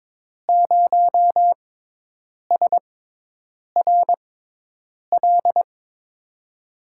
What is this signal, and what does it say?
Morse code, heard as a keyed tone, about 22 wpm, spaced as for 8 wpm, 700 Hz. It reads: 0SRL
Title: USA Call Signs - 5 Characters - 6x Character Spacing - 22wpm